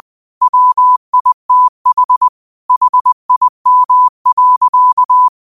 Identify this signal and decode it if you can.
Morse code, heard as a keyed tone, about 20 wpm, 1000 Hz. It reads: WITH HIM.
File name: qotdfiles / 20wpm